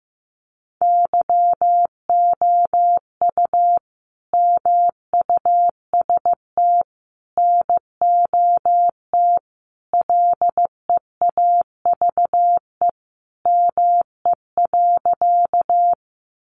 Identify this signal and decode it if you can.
Morse code, heard as a keyed tone, about 15 wpm, 700 Hz. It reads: YOU MUST NOT LEAVE ME.
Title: Sentences from Top 600 Words 15wpm